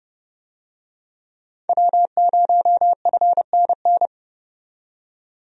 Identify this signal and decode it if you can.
Morse code, heard as a keyed tone, about 30 wpm, 700 Hz. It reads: W0FNN
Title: USA Call Signs - 5 Characters - 30wpm